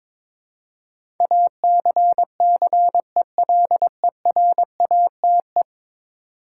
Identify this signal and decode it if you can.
Morse code, heard as a keyed tone, about 22 wpm, 700 Hz. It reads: ACCELERATE
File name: practice / 22wpm / 10 Letter Words - First 3 Letters Given - 22wpm